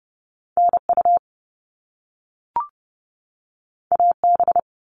Morse code, heard as a keyed tone, about 30 wpm, 700 Hz. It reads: NU AB
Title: Top Two Letter Combinations in Words 30wpm